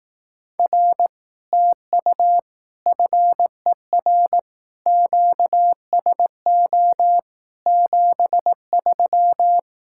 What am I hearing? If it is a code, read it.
Morse code, heard as a keyed tone, about 18 wpm, 700 Hz. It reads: R TU FER QSO 73
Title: Speed-Racing Common QSO Phrases Encore - 15wpm